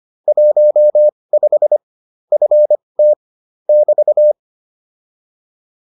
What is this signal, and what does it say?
Morse code, heard as a keyed tone, about 25 wpm, spaced as for 20 wpm, 600 Hz. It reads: 15 FT <BT>